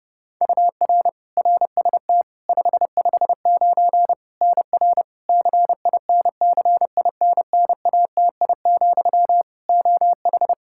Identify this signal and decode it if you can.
Morse code, heard as a keyed tone, about 30 wpm, 700 Hz. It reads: UR RST 559 NR CINCINNATI, OH
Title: K1USN SST QSO Practice 30wpm